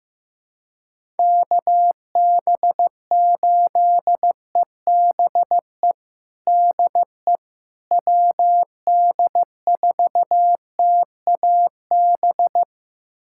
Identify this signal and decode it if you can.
Morse code, heard as a keyed tone, about 15 wpm, 700 Hz. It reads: KB8EBE DE WD4TAB
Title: USA Callsign DE Callsign - 6 Characters - 15wpm